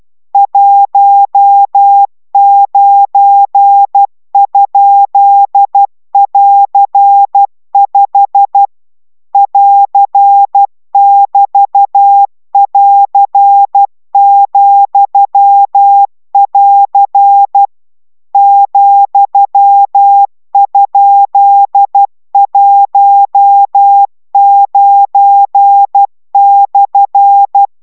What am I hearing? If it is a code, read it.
Morse code, heard as a keyed tone, about 12 wpm, 800 Hz. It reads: 19?<AR>5 <AR><BT><AR>,<AR> ,?19/